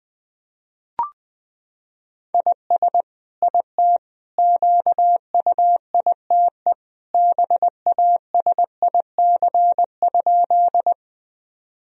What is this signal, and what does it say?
Morse code, heard as a keyed tone, about 21 wpm, 700 Hz. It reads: IS IT QUITE BASIC?
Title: Sentences from Top 1000 Words - Review Part 3 - 20wpm